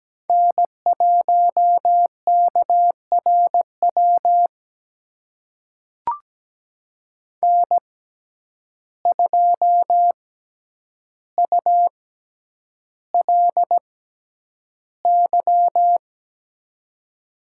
Morse code, heard as a keyed tone, about 17 wpm, spaced as for 9 wpm, 700 Hz. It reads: N1KRW N2ULY
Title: USA Call Signs - 5 Characters - 6x Character Spacing - 17wpm